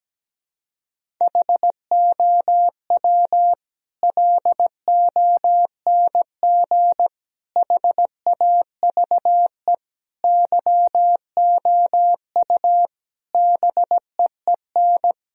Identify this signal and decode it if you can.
Morse code, heard as a keyed tone, about 17 wpm, 700 Hz. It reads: HOW LONG HAVE YOU BEEN
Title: Sentences from Top 1000 Words - Review Part 3 - 17wpm